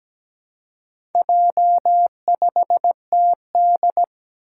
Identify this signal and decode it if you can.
Morse code, heard as a keyed tone, about 17 wpm, 700 Hz. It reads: J5TD
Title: Four Challenging Letter-Number Combinations 17wpm